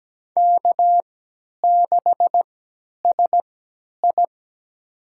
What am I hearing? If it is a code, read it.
Morse code, heard as a keyed tone, about 17 wpm, spaced as for 10 wpm, 700 Hz. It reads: K6SI
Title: USA Call Signs - 3x Character Spacing - 17wpm